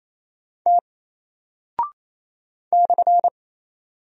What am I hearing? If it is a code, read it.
Morse code, heard as a keyed tone, about 27 wpm, 700 Hz. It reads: T /
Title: Single Character (Letter:Number:Punctuation:Prosign) Alt 28wpm